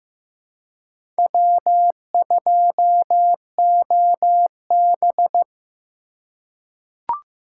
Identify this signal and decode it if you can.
Morse code, heard as a keyed tone, about 15 wpm, 700 Hz. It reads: W2OB
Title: USA Call Signs - 4 Characters - 2x Character Spacing - 15wpm